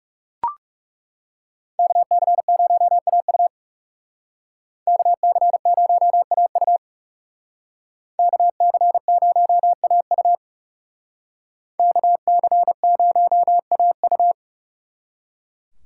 Morse code, heard as a keyed tone, about 36 wpm, 700 Hz. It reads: KC0AU KC0AU KC0AU KC0AU
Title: Speed Racing - USA 2x2 Call Signs 30wpm